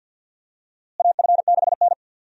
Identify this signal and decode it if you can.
Morse code, heard as a keyed tone, about 50 wpm, 700 Hz. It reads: AF6N
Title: USA Call Signs - 4 Characters - Repeated 2x - 50wpm